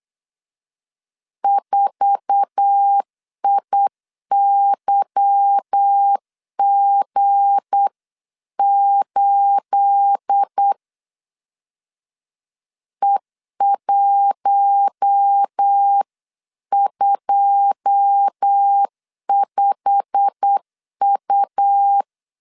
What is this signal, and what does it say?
Morse code, heard as a keyed tone, about 8 wpm, 800 Hz. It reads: 4IYG8 E125U
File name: CW / DET / 1 9 1500C S B 8wpm 5624